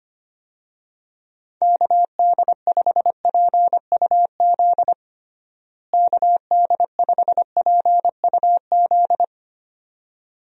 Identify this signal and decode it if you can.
Morse code, heard as a keyed tone, about 25 wpm, 700 Hz. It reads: KD5PUZ KD5PUZ
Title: USA Call Signs - 6 Characters - Repeated 2x - 25wpm